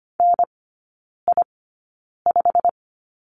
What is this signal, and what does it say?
Morse code, heard as a keyed tone, about 25 wpm, spaced as for 9 wpm, 690 Hz. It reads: NI5
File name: Lesson 15e - Number 5 - Three Characters 25wpm